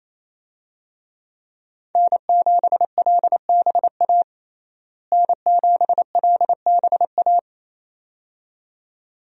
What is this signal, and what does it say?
Morse code, heard as a keyed tone, about 28 wpm, 700 Hz. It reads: N7LBA N7LBA
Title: USA Call Signs - 5 Characters - Repeated 2x - 28wpm